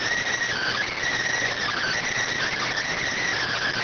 Emergency Alert System (SAME) tones, heard as harsh buzzing bursts. It is heard from 0.5 to 3.3 s.